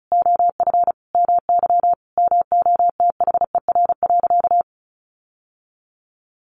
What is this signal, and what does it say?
Morse code, heard as a keyed tone, about 34 wpm, 700 Hz. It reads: OF MY MOTHER.
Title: Sentences from Top 1400 Words 35wpm